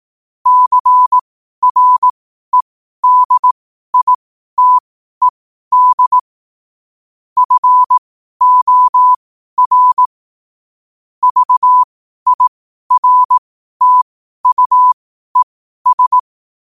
Morse code, heard as a keyed tone, about 18 wpm, spaced as for 12 wpm, 1000 Hz. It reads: CREDITED FOR VIRTUES
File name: qotdfiles / 12wpm